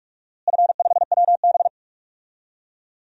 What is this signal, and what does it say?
Morse code, heard as a keyed tone, about 41 wpm, 700 Hz. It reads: F5WB
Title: DXCC-100 Call Signs - 4 Characters - Repeated 2x - 45wpm